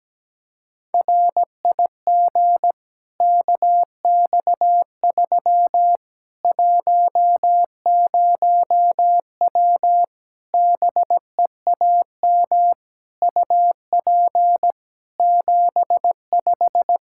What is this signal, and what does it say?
Morse code, heard as a keyed tone, about 17 wpm, 700 Hz. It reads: RIG KX3 10W BEAM UP 75
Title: Practice QSO - Exchange 2 - Info - 17wpm